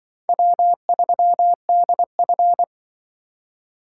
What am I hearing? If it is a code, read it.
Morse code, heard as a keyed tone, about 24 wpm, 700 Hz. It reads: W3DF
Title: Speed Racing - USA 1x2 Call Signs 20wpm